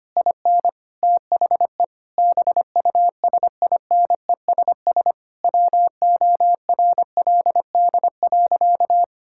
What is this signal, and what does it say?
Morse code, heard as a keyed tone, about 25 wpm, 700 Hz. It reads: IN THE BUSINESS WORLD.
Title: Sentences from Top 1000 Words - Review Part 5 - 25wpm